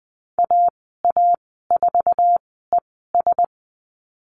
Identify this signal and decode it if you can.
Morse code, heard as a keyed tone, about 20 wpm, spaced as for 14 wpm, 700 Hz. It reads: AA4ES
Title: 076 Lesson 13g - Letter L - Call Signs 20wpm